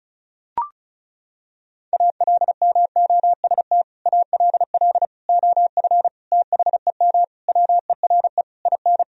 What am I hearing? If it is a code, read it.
Morse code, heard as a keyed tone, about 36 wpm, 700 Hz. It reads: ALMOST ALL OF THEM WERE IN
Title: Sentences from Top 1000 Words - Review Part 4 - 35wpm